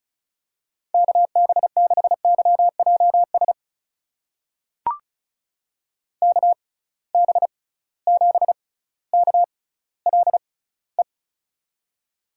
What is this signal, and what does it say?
Morse code, heard as a keyed tone, about 36 wpm, spaced as for 18 wpm, 690 Hz. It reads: KB6YJS KB7KLE
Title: USA Call Signs - 6 Characters - 6x Character Spacing - 35wpm